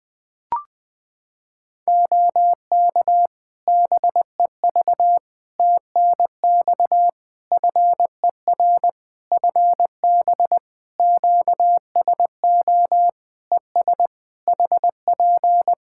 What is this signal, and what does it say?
Morse code, heard as a keyed tone, about 20 wpm, 700 Hz. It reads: OK BEV TNX FER FB QSO ES HP
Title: Practice QSO - Exchange 4 - Ending - 20wpm